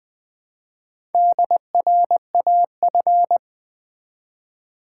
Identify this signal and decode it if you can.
Morse code, heard as a keyed tone, about 20 wpm, 700 Hz. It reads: DRAF
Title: Build A Word - 1 Letter At A Time - 5-Letter Words - 20wpm